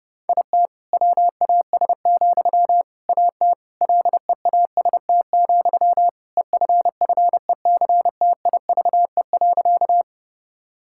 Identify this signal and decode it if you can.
Morse code, heard as a keyed tone, about 32 wpm, 700 Hz. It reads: IT WAS, AT LEAST, EFFECTIVE.
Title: Sentences from Top 1000 Words - Review Part 6 - 30wpm